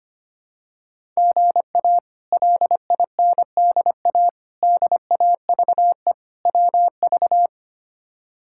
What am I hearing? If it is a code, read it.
Morse code, heard as a keyed tone, about 25 wpm, 700 Hz. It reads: GA LINDA DAVE WV
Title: K1USN SST QSO Practice - 25wpm